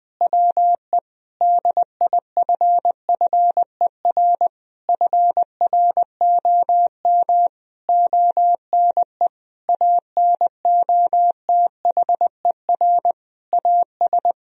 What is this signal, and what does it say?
Morse code, heard as a keyed tone, about 20 wpm, 700 Hz. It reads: WE DIFFER FROM ONE ANOTHER AS
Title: Sentences from Top 1000 Words - Review Part 1 - 20wpm